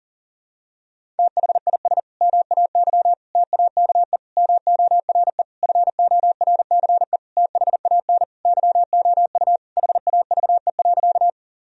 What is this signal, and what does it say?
Morse code, heard as a keyed tone, about 40 wpm, 700 Hz. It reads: THIS MAY TAKE MORE FORCE THAN YOU HAVE.
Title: Sentences from Top 1000 Words - Review Part 3 - 40wpm